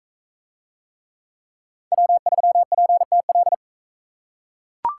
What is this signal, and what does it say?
Morse code, heard as a keyed tone, about 43 wpm, 700 Hz. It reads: W3PTL